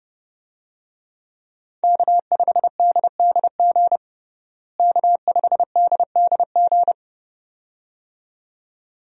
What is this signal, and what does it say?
Morse code, heard as a keyed tone, about 30 wpm, 700 Hz. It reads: K5DDG K5DDG
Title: USA Call Signs - 5 Characters - Repeated 2x - 30wpm